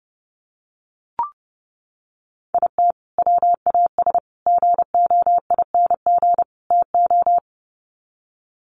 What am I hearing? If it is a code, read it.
Morse code, heard as a keyed tone, about 30 wpm, 700 Hz. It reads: IT WAS GOING TO
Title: Sets of 4 Words 30wpm